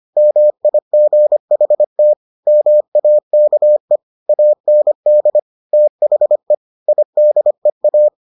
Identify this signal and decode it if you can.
Morse code, heard as a keyed tone, about 25 wpm, 600 Hz. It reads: MIGHT MAKE AND THE IDEA